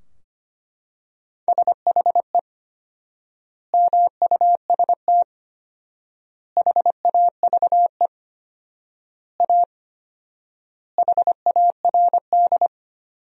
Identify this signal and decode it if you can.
Morse code, heard as a keyed tone, about 25 wpm, 700 Hz. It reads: SHE MUST HAVE A HARD
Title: 4x Word-Spacing - Sentences from Top 300 Words - 25wpm